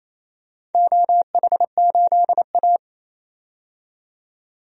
Morse code, heard as a keyed tone, about 27 wpm, 700 Hz. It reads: OH8A
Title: DXCC-100 Call Signs - 4 Characters - 28wpm